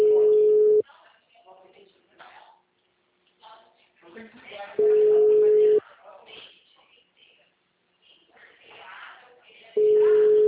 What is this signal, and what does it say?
Ringback tone (Europe, 425 Hz), with speech.